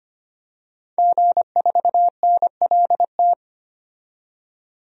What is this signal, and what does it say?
Morse code, heard as a keyed tone, about 25 wpm, 700 Hz. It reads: G4NLT
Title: DXCC-100 Call Signs - 5 Characters - 25wpm